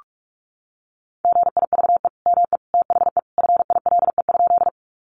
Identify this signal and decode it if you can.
Morse code, heard as a keyed tone, about 44 wpm, 700 Hz. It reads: GIVE ME THE FILE?
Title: Sentences from Top 1200 Words 45wpm